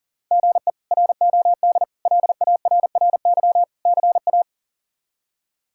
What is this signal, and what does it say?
Morse code, heard as a keyed tone, about 37 wpm, 700 Hz. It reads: GE ROD LARRY CA
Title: K1USN SST QSO Practice - 40wpm